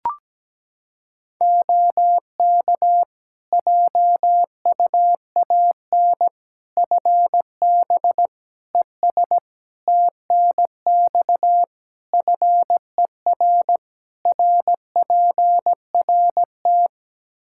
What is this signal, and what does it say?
Morse code, heard as a keyed tone, about 17 wpm, 700 Hz. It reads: OK JUAN FB ES TNX FER RPRT
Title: K1USN SST QSO Practice 17wpm